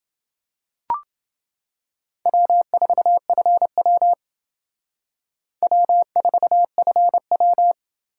Morse code, heard as a keyed tone, about 29 wpm, 690 Hz. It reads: W4FW W4FW